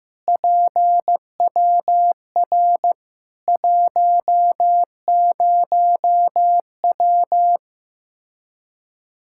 Morse code, heard as a keyed tone, about 16 wpm, 700 Hz. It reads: PWR 10W
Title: Practice QSO - Exchange 2 - Info - 15wpm